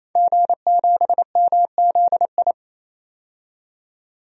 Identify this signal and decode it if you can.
Morse code, heard as a keyed tone, about 28 wpm, 700 Hz. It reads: G7MZI
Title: DXCC-100 Call Signs - 5 Characters - 28wpm